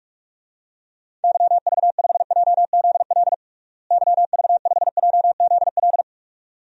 Morse code, heard as a keyed tone, about 46 wpm, 700 Hz. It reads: YV5JZL YV5JZL
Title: DXCC-100 Call Signs - 6 Characters - Repeated 2x - 45wpm